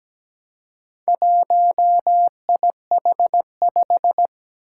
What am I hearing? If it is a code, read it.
Morse code, heard as a keyed tone, about 17 wpm, 700 Hz. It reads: 1IH5
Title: Four Challenging Letter-Number Combinations 17wpm